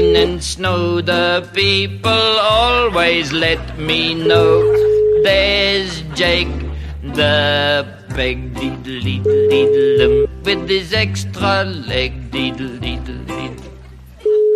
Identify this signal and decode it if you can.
Ringback tone (Europe, 425 Hz), with speech.